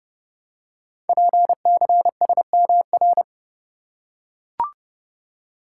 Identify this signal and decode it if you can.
Morse code, heard as a keyed tone, about 30 wpm, 700 Hz. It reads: PCSMR
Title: Code Groups - 5 Letter:Numbers - 30wpm